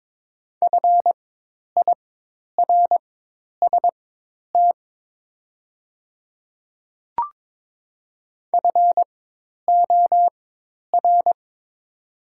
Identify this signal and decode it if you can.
Morse code, heard as a keyed tone, about 23 wpm, spaced as for 10 wpm, 700 Hz. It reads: FIRST FOR
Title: 4x Character Spacing - Top 100 Words - 22wpm